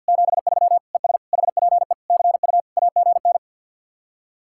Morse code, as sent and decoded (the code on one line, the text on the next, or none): --... ...-- / . ... / .... .--. / -.-. ..- / .- --. -.
73 ES HPE CU AGN